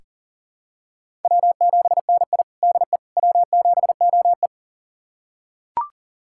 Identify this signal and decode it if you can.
Morse code, heard as a keyed tone, about 40 wpm, 700 Hz. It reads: W7NI DE W7OE
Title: USA Callsign DE Callsign - 4 Characters - 40wpm